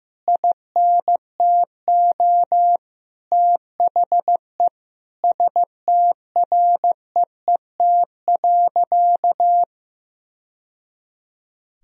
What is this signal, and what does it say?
Morse code, heard as a keyed tone, about 15 wpm, 700 Hz. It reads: INTO THE STREET.
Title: Sentences from Top 1000 Words - Review Part 3 - 15wpm